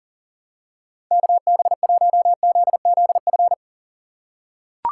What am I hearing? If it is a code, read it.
Morse code, heard as a keyed tone, about 40 wpm, 700 Hz. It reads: KB1ZZF
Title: USA Call Signs - 3x Character Spacing - 40wpm